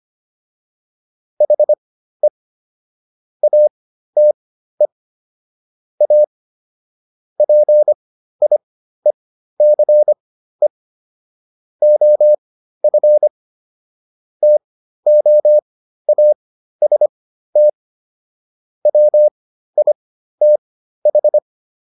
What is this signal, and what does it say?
Morse code, heard as a keyed tone, about 25 wpm, spaced as for 13 wpm, 600 Hz. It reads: HE ATE A PIECE OF TOAST WITH